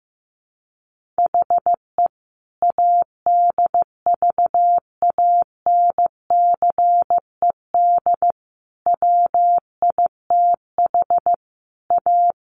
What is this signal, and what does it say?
Morse code, heard as a keyed tone, about 15 wpm, 700 Hz. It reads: HE ADVANCED WITH A
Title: Sentences from Top 1400 Words 15wpm